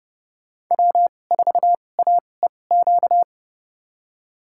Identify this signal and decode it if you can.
Morse code, heard as a keyed tone, about 30 wpm, spaced as for 22 wpm, 700 Hz. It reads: W4AEQ
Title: USA Call Signs - 5 Characters - 2x Character Spacing - 30wpm